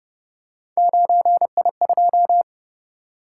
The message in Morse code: 9I2

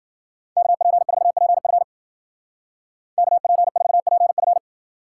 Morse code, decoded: KP4PF KP4PF

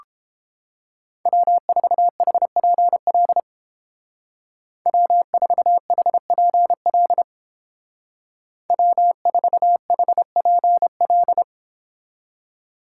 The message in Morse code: W4HPL W4HPL W4HPL

